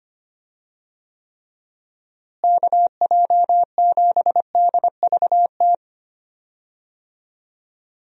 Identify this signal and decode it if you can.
Morse code, heard as a keyed tone, about 25 wpm, 700 Hz. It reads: KJ7DVT
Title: USA Call Signs - 6 Characters - Repeated 3x - 25wpm